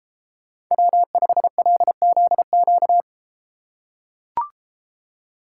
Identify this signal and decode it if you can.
Morse code, heard as a keyed tone, about 31 wpm, 700 Hz. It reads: W5LZQ